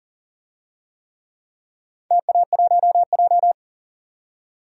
Morse code, heard as a keyed tone, about 41 wpm, 700 Hz. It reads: TA1J
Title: DXCC-100 Call Signs - 4 Characters - 40wpm